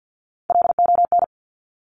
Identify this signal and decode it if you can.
Morse code, heard as a keyed tone, about 50 wpm, 700 Hz. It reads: LON
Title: Top Three Letter Combinations in Words 50wpm